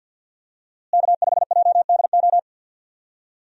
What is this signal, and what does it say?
Morse code, heard as a keyed tone, about 46 wpm, 700 Hz. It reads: K5JDO
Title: USA Call Signs - 5 Characters - Repeated 3x - 50wpm